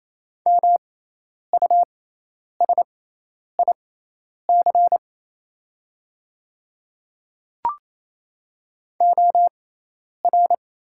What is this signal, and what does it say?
Morse code, heard as a keyed tone, about 28 wpm, spaced as for 10 wpm, 700 Hz. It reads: MUSIC OR
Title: 6x Character Spacing - Top 500 Words - 28wpm